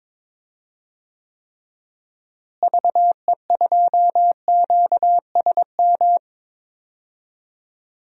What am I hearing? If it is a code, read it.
Morse code, heard as a keyed tone, about 22 wpm, 700 Hz. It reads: VE2QSM
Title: DXCC-100 Call Signs - 6 Characters - Repeated 2x - 22wpm